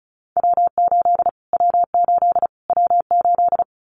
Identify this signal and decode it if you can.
Morse code, heard as a keyed tone, about 34 wpm, 700 Hz. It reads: W8 W8 W8
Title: Callsign Prefixes for Top 25 DXCC - Repeated 3x - 35wpm